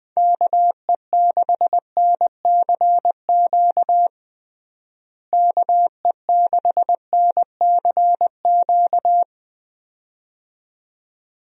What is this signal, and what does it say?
Morse code, heard as a keyed tone, about 20 wpm, 700 Hz. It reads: KE6NCQ KE6NCQ
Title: USA Call Signs - 6 Characters - Repeated 2x - 20wpm